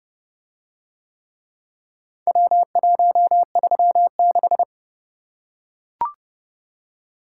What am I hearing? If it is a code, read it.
Morse code, heard as a keyed tone, about 30 wpm, 700 Hz. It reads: W136